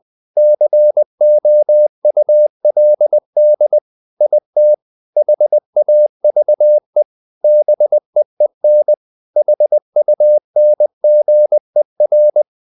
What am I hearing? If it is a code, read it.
Morse code, heard as a keyed tone, about 20 wpm, 600 Hz. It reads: COULD IT HAVE BEEN HUNGER
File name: the old man and the sea part 2 20wpm